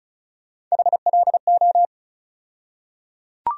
Morse code, heard as a keyed tone, about 35 wpm, 700 Hz. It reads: HLO
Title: USA Call Sign Postfixes - 3 Letters 35wpm